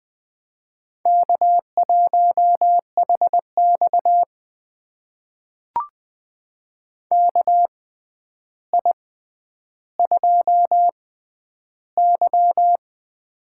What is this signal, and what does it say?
Morse code, heard as a keyed tone, about 20 wpm, spaced as for 10 wpm, 700 Hz. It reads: K1HX KI2Y
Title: USA Call Signs - 4 Characters - 6x Character Spacing - 20wpm